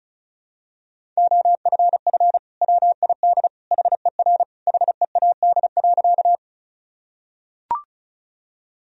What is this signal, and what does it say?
Morse code, heard as a keyed tone, about 36 wpm, 700 Hz. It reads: OFF WID HER HEAD.
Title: Sentences from Top 300 Words and Abbreviations 35wpm